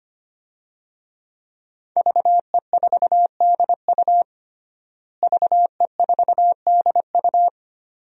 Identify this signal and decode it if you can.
Morse code, heard as a keyed tone, about 25 wpm, 700 Hz. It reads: VE4DU VE4DU
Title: DXCC-100 Call Signs - 5 Characters - Repeated 2x - 25wpm